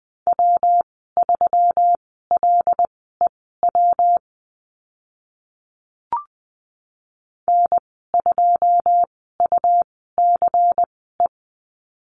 Morse code, heard as a keyed tone, about 19 wpm, spaced as for 14 wpm, 700 Hz. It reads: W3LEW N2UCE
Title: 140 Lesson 23g - Letter f - Call Signs 20wpm